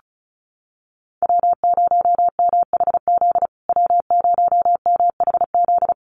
Morse code, heard as a keyed tone, about 34 wpm, 700 Hz. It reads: W0MHZ W0MHZ
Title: USA Call Signs Repeated 3x 35wpm